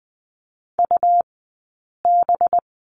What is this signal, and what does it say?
Morse code, heard as a keyed tone, about 20 wpm, 700 Hz. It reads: U B
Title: 183 Lesson 30d - Letter b - Two Characters 20wpm